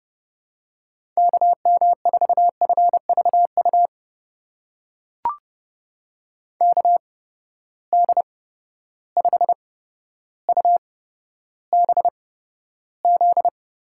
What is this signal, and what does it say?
Morse code, heard as a keyed tone, about 30 wpm, spaced as for 13 wpm, 700 Hz. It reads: KM4FVU KD5UBZ
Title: USA Call Signs - 6 Characters - 8x Character Spacing - 30wpm